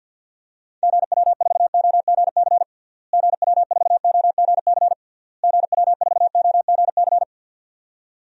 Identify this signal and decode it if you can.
Morse code, heard as a keyed tone, about 52 wpm, 700 Hz. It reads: GW4OGC GW4OGC GW4OGC
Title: DXCC-100 Call Signs - 6 Characters - Repeated 3x - 50wpm